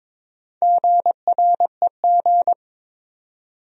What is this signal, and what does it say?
Morse code, heard as a keyed tone, about 22 wpm, 700 Hz. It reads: GREG
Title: USA Common Names - Max 5 Letters - 22wpm